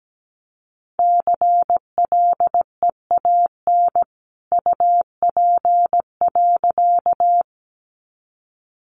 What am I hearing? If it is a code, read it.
Morse code, heard as a keyed tone, about 17 wpm, 700 Hz. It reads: CLEAN UP.
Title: A Sound Of Thunder 17wpm